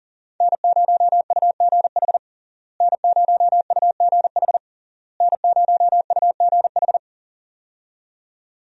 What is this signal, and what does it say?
Morse code, heard as a keyed tone, about 40 wpm, 700 Hz. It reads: N0UGH N0UGH N0UGH
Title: USA Call Signs - 5 Characters - Repeated 3x - 40wpm